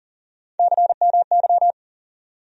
Morse code, heard as a keyed tone, about 40 wpm, 700 Hz. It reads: CMY
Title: USA Call Sign Postfixes - 3 Letters 40wpm